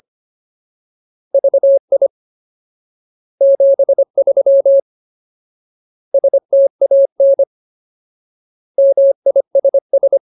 Morse code, heard as a keyed tone, about 25 wpm, 550 Hz. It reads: VI 73 STAN MISS
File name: Ultimate Word List / 25WPM Ultimate